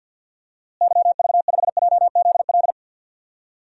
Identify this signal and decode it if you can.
Morse code, heard as a keyed tone, about 53 wpm, 700 Hz. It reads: YV5JZL
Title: DXCC-100 Call Signs - 6 Characters - Repeated 2x - 50wpm